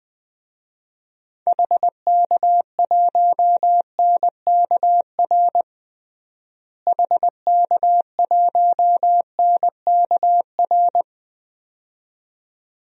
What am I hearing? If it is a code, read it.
Morse code, heard as a keyed tone, about 21 wpm, 690 Hz. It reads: HK1NKR HK1NKR
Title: DXCC-100 Call Signs - 6 Characters - Repeated 2x - 20wpm